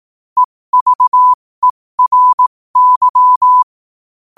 Morse code, heard as a keyed tone, about 18 wpm, spaced as for 15 wpm, 1000 Hz. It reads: EVERY
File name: qotdfiles / 15wpm